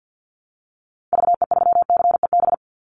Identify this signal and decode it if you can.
Morse code, heard as a keyed tone, about 48 wpm, 700 Hz. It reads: VE3CEB
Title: Call Signs Repeated 2x - 50wpm